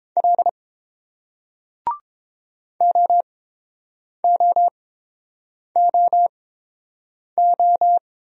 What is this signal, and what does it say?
Morse code, heard as a keyed tone, about 31 wpm, spaced as for 7 wpm, 700 Hz. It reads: L OOOO